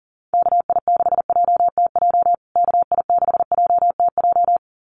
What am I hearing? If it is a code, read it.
Morse code, heard as a keyed tone, about 40 wpm, 700 Hz. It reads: KI6JTJ KI6JTJ
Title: USA Call Signs Repeated 2x 40wpm